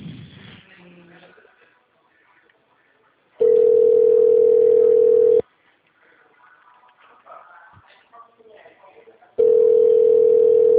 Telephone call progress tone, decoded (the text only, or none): ringback tone